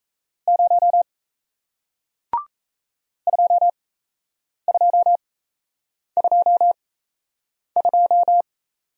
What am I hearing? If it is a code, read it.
Morse code, heard as a keyed tone, about 38 wpm, spaced as for 8 wpm, 700 Hz. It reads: 0 2222